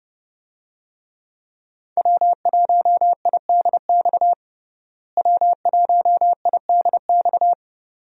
Morse code, heard as a keyed tone, about 30 wpm, 700 Hz. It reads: W1IDX W1IDX
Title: USA Call Signs - 5 Characters - Repeated 2x - 30wpm